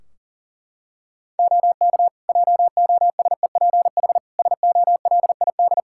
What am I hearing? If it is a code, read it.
Morse code, heard as a keyed tone, about 40 wpm, 700 Hz. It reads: OK JOSEPH SOLID
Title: Practice QSO - Exchange 3 - More Info - 40wpm